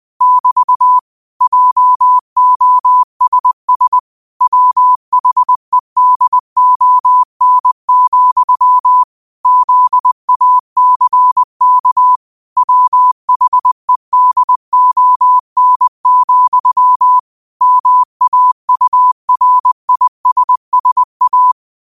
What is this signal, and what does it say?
Morse code, heard as a keyed tone, about 20 wpm, 1000 Hz. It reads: <BT> JOSS WHEDON, ZACK WHEDON, MAURISSA
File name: qotdfiles / 20wpm